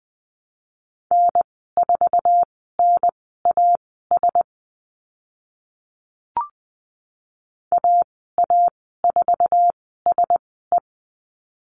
Morse code, heard as a keyed tone, about 20 wpm, spaced as for 15 wpm, 700 Hz. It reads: N4NAS AA4SE
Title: 048 Lesson 9g - Number 4 - Call Signs 20wpm